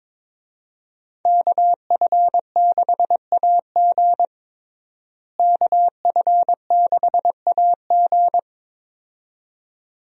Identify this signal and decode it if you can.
Morse code, heard as a keyed tone, about 22 wpm, 700 Hz. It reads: KF6AG KF6AG
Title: USA Call Signs - 5 Characters - Repeated 2x - 22wpm